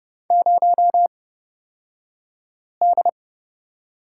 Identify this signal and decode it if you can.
Morse code, heard as a keyed tone, about 30 wpm, spaced as for 5 wpm, 700 Hz. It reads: 0D
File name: Lesson 38d - Letter q - Two Characters 30wpm